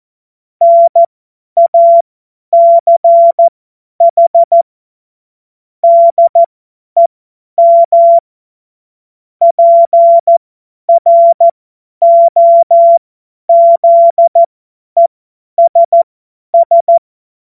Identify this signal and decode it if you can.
Morse code, heard as a keyed tone, about 14 wpm, spaced as for 10 wpm, 680 Hz. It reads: NACH DEM PROZESS
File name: Morsekurs / Witze / Witz 03. Mai 2022 50 F70